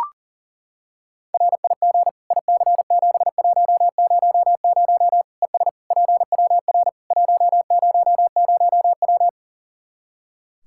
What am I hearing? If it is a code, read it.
Morse code, heard as a keyed tone, about 40 wpm, 700 Hz. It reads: RIG IC7100 ES PWR 100W
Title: Practice QSO - Exchange 2 - Info - 40wpm